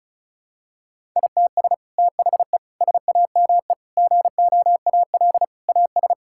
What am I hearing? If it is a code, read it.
Morse code, heard as a keyed tone, about 36 wpm, 700 Hz. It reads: ITS THE SAME GOAL AS
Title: Sentences from Top 1000 Words - Review Part 5 - 35wpm